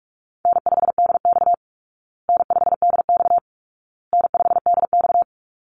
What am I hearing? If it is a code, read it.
Morse code, heard as a keyed tone, about 44 wpm, 700 Hz. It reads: N5DX N5DX N5DX
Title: Easy Call Signs Repeated 3x - 45wpm